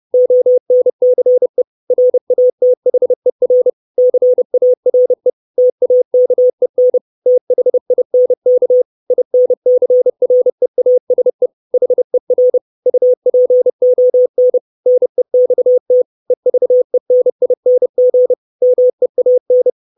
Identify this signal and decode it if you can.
Morse code, heard as a keyed tone, about 30 wpm, 500 Hz. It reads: ONCE RATHER CARE TAKEN THINK INCREASE HER UPON NEXT EVENING MEAN
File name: 632 Words 30WPM